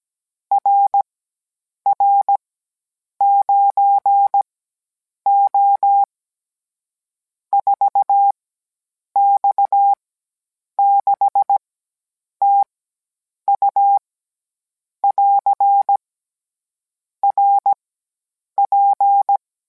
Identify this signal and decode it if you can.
Morse code, heard as a keyed tone, about 17 wpm, spaced as for 8 wpm, 800 Hz. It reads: RR9O 4X6TU <AR> RP